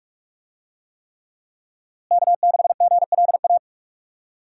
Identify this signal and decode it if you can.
Morse code, heard as a keyed tone, about 46 wpm, 700 Hz. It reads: K6GLA
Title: USA Call Signs - 5 Characters - Repeated 3x - 45wpm